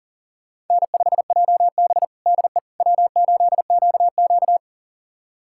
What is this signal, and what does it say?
Morse code, heard as a keyed tone, about 40 wpm, 700 Hz. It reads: N5JB DE W8QQ
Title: USA Callsign DE Callsign - 4 Characters - 40wpm